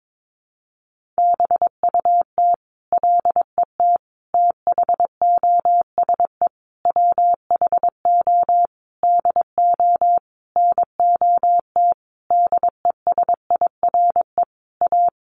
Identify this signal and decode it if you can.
Morse code, heard as a keyed tone, about 22 wpm, 700 Hz. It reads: BUT LET THOSE WHO DO NOT DESIRE A
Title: Sentences from Top 1500 Words 22wpm